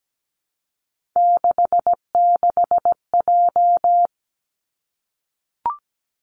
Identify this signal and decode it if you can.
Morse code, heard as a keyed tone, about 17 wpm, 700 Hz. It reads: 66J
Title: Three Challenging Letter-Number Combinations 17wpm